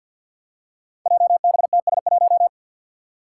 Morse code, heard as a keyed tone, about 48 wpm, 700 Hz. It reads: JBTS1
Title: Five Challenging Letter-Number Combinations 50wpm